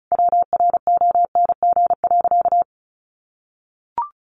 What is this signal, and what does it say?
Morse code, heard as a keyed tone, about 34 wpm, 700 Hz. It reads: WRONG.
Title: Easy Phrases 35wpm